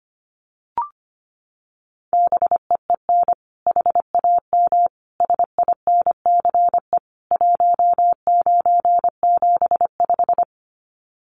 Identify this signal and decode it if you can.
Morse code, heard as a keyed tone, about 25 wpm, 700 Hz. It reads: BEEN HAM SINCE 1975
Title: Common QSO Phrases 25wpm